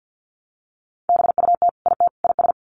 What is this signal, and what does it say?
Morse code, heard as a keyed tone, about 48 wpm, 700 Hz. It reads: BUT IT IS
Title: Sets of 3 Words 50wpm